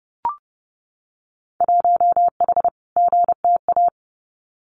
Morse code, heard as a keyed tone, about 29 wpm, 700 Hz. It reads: 1H GTA
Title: Winter Field Day Exchanges 30wpm